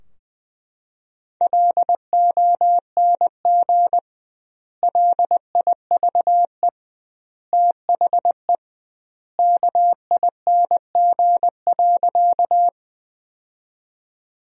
Morse code, heard as a keyed tone, about 20 wpm, 700 Hz. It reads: LONG LIVE THE KING.
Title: 2x Word-Spacing - Sentences from Top 400 Words - 20wpm